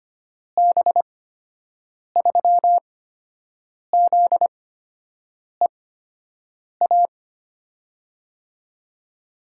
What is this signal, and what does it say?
Morse code, heard as a keyed tone, about 25 wpm, spaced as for 7 wpm, 700 Hz. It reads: B3ZEA
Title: USA Call Signs - 6 Characters - 8x Character Spacing - 25wpm